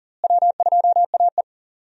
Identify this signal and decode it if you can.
Morse code, heard as a keyed tone, about 40 wpm, 700 Hz. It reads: W2AE
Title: USA Call Signs - 4 Characters - Repeated 2x - 40wpm